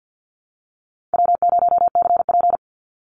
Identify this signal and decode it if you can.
Morse code, heard as a keyed tone, about 50 wpm, 700 Hz. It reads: W0CP
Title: NA SOTA Activator Callsigns 50wpm